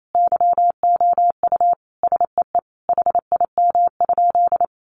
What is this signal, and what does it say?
Morse code, heard as a keyed tone, about 28 wpm, 700 Hz. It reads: YOU SEE HIM?
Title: Sentences from Top 100 Words 28wpm